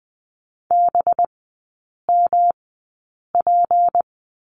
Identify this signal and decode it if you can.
Morse code, heard as a keyed tone, about 20 wpm, spaced as for 8 wpm, 700 Hz. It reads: BMP
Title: 184 Lesson 30e - Letter b - Three Characters 20wpm